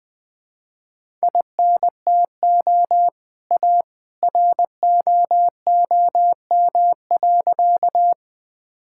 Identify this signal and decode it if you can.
Morse code, heard as a keyed tone, about 20 wpm, 700 Hz. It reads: INTO A ROOM.
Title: Sentences with 2-Word Phrases - 20wpm